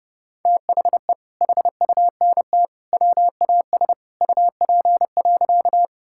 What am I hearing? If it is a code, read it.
Morse code, heard as a keyed tone, about 32 wpm, 700 Hz. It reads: THE HUNT WAS UP.
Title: Sentences from Top 1600 Words 30wpm